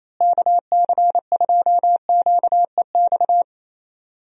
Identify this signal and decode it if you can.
Morse code, heard as a keyed tone, about 28 wpm, 700 Hz. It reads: KC2QEX
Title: USA Call Signs - 6 Characters - Repeated 2x - 28wpm